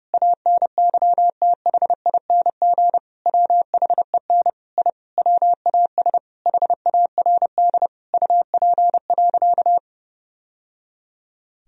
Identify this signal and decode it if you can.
Morse code, heard as a keyed tone, about 30 wpm, 700 Hz. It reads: ANYTHING WHEN I WAS HARD UP.
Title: Sentences from Top 1000 Words - Review Part 1 - 30wpm